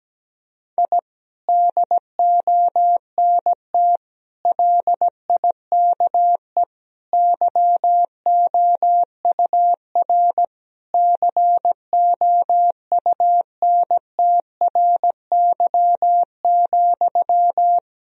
Morse code, heard as a keyed tone, about 17 wpm, 700 Hz. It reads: I DONT LIKE YOUR COUNTRY,
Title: Sentences from Top 1000 Words - Review Part 3 - 17wpm